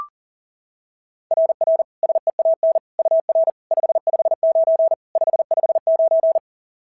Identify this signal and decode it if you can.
Morse code, heard as a keyed tone, about 37 wpm, 690 Hz. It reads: RR SEAN UR 559 559
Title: SOTA QSO Practice 40wpm